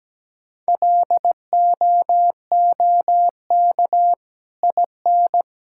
Morse code, heard as a keyed tone, about 17 wpm, 700 Hz. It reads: LOOK IN
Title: Sentences from Top 1600 Words 17wpm